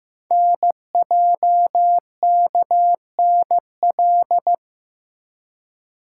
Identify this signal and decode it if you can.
Morse code, heard as a keyed tone, about 15 wpm, 700 Hz. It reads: NJKNL
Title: Code Groups - 5 Letter:Numbers - 15wpm